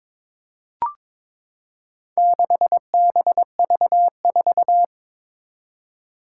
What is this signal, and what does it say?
Morse code, heard as a keyed tone, about 22 wpm, 700 Hz. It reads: 6BV4
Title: Four Challenging Letter-Number Combinations 22wpm